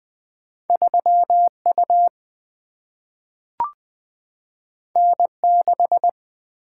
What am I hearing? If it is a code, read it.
Morse code, heard as a keyed tone, about 20 wpm, 700 Hz. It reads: 3U N6
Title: Single-Letter Single-Number 20wpm